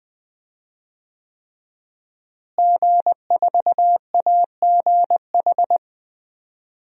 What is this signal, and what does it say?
Morse code, heard as a keyed tone, about 21 wpm, 700 Hz. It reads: G4AGH
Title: DXCC-100 Call Signs - 5 Characters - Repeated 3x - 20wpm